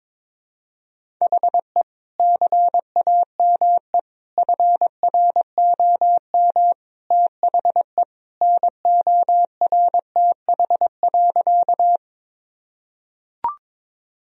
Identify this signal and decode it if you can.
Morse code, heard as a keyed tone, about 22 wpm, 700 Hz. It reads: HE CAME FROM THE NORTH.
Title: Sentences from Top 1000 Words - Review Part 1 - 22wpm